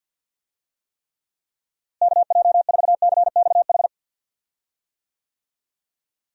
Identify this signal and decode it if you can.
Morse code, heard as a keyed tone, about 52 wpm, 700 Hz. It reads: KJ4CXH
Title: USA Call Signs - 6 Characters - Repeated 3x - 50wpm